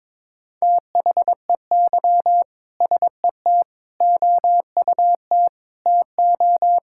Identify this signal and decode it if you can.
Morse code, heard as a keyed tone, about 22 wpm, 700 Hz. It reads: THEY SET OUT TO
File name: practice / 22wpm / Sentences with 2-Word Phrases - 22wpm